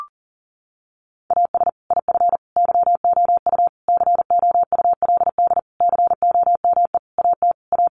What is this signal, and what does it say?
Morse code, heard as a keyed tone, about 40 wpm, 700 Hz. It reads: AS IF YOU COULD COME AT A
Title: Sentences from Top 900 Words 40wpm